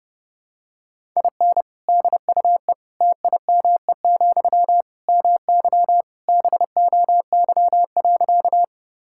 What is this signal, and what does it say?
Morse code, heard as a keyed tone, about 30 wpm, 700 Hz. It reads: IN DUE TIME, MY BOY.
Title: Sentences from Top 1000 Words - Review Part 6 - 30wpm